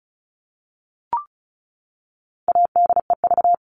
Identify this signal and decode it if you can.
Morse code, heard as a keyed tone, about 34 wpm, 700 Hz. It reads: ADEV
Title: All Cut Numbers - 4-Digit 35wpm